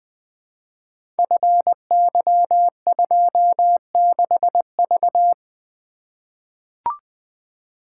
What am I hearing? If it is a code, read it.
Morse code, heard as a keyed tone, about 20 wpm, 700 Hz. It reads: FY26V
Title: Code Groups - 5 Letter:Numbers - 20wpm